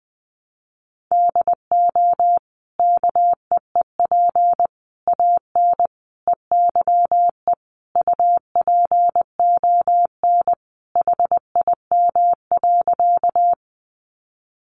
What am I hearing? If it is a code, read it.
Morse code, heard as a keyed tone, about 20 wpm, 700 Hz. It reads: DO KEEP AN EYE UPON HIM.
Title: Sentences from Top 1100 Words 20wpm